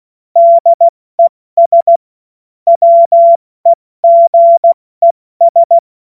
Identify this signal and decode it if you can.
Morse code, heard as a keyed tone, about 16 wpm, 680 Hz. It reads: DES WEGES
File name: Morsekurs / Witze / Witz 16. Dezember 2025 70 F80